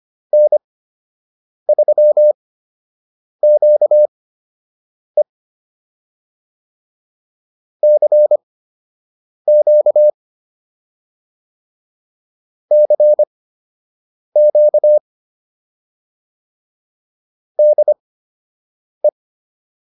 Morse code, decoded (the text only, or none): N3QE CQ CQ DE